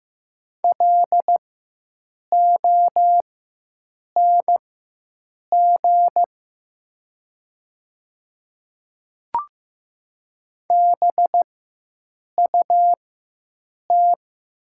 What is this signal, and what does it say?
Morse code, heard as a keyed tone, about 15 wpm, spaced as for 7 wpm, 700 Hz. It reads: LONG BUT
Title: 4x Character Spacing - Top 100 Words - 15wpm